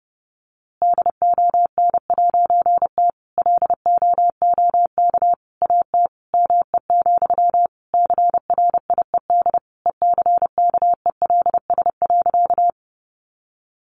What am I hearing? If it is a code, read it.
Morse code, heard as a keyed tone, about 28 wpm, 700 Hz. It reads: DON'T LOOK AT ME, CRIED ECKELS.
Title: A Sound Of Thunder 30wpm